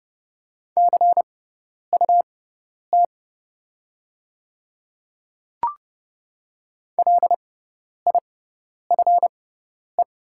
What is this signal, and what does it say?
Morse code, heard as a keyed tone, about 30 wpm, spaced as for 10 wpm, 700 Hz. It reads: CUT LIFE
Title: 6x Character Spacing - Top 300 Words - 30wpm